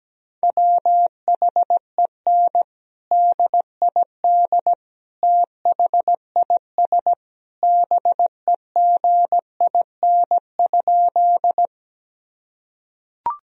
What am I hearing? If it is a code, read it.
Morse code, heard as a keyed tone, about 17 wpm, 700 Hz. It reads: WHEN DID THIS BEGIN?
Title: Sentences from Top 1000 Words - Review Part 1 - 17wpm